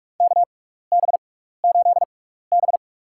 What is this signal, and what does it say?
Morse code, heard as a keyed tone, about 44 wpm, spaced as for 15 wpm, 700 Hz. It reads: KB8B
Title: USA Call Signs - 4 Characters - 6x Character Spacing - 45wpm